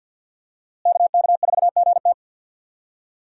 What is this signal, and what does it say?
Morse code, heard as a keyed tone, about 49 wpm, 700 Hz. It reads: KK4GT
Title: USA Call Signs - 5 Characters - Repeated 2x - 50wpm